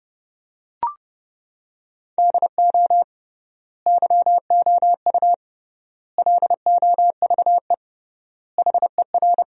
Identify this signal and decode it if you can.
Morse code, heard as a keyed tone, about 32 wpm, 700 Hz. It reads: DO YOU LOVE HER
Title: 3x Word-Spacing - Sentences from Top 400 Words - 30wpm